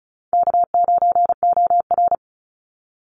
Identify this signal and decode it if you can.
Morse code, heard as a keyed tone, about 34 wpm, 700 Hz. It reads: K9OR
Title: Easy Call Signs Repeated 2x - 35wpm